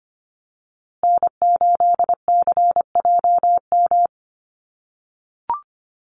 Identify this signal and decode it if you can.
Morse code, heard as a keyed tone, about 25 wpm, 700 Hz. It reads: N8CJM
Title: USA Call Signs Repeated 3x 25wpm